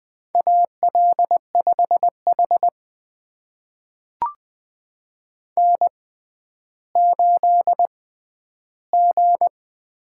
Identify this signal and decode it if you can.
Morse code, heard as a keyed tone, about 21 wpm, spaced as for 12 wpm, 700 Hz. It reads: AL5H N8G
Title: USA Call Signs - 6x Character Spacing - 20wpm